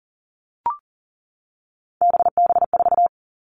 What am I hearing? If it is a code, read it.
Morse code, heard as a keyed tone, about 40 wpm, 700 Hz. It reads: BB4
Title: All Cut Numbers - 3-Digit 40wpm